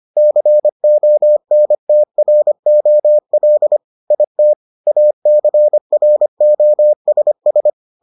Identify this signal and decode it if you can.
Morse code, heard as a keyed tone, about 25 wpm, 600 Hz. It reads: CONTROL IT ACROSS